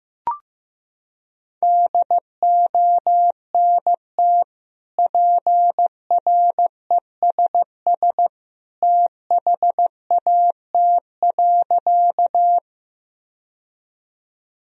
Morse code, heard as a keyed tone, about 15 wpm, 700 Hz. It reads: DONT PRESS THAT.
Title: Sentences from Top 1000 Words - Review Part 1 - 15wpm